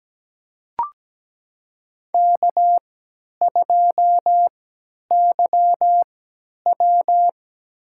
Morse code, heard as a keyed tone, about 17 wpm, spaced as for 10 wpm, 700 Hz. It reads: K2YW
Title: USA Call Signs - 4 Characters - 3x Character Spacing - 17wpm